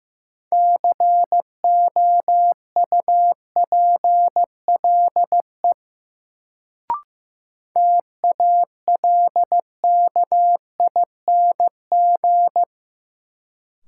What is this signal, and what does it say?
Morse code, heard as a keyed tone, about 15 wpm, 700 Hz. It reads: COUPLE TALKING
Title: Sets of 2 Words - New Word Drill - 15wpm